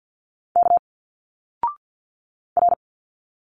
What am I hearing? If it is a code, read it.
Morse code, heard as a keyed tone, about 52 wpm, 700 Hz. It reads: K R
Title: Single Letters 50wpm